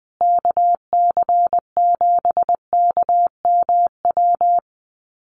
Morse code, heard as a keyed tone, about 20 wpm, 700 Hz. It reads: KC7KMW
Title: USA Call Signs Repeated 3x 20wpm